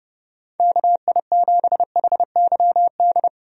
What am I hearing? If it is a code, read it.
Morse code, heard as a keyed tone, about 30 wpm, 700 Hz. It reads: KI7HYD
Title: USA Call Signs - 6 Characters - Repeated 3x - 30wpm